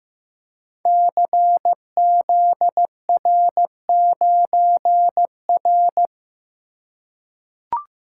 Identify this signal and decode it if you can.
Morse code, heard as a keyed tone, about 15 wpm, 700 Hz. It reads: CZR9R
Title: Code Groups - 5 Letter:Numbers - 15wpm